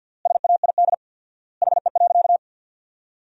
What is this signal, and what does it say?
Morse code, heard as a keyed tone, about 53 wpm, 700 Hz. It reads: SAID HE.
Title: 4x Word-Spacing - Sentences from Top 400 Words - 50wpm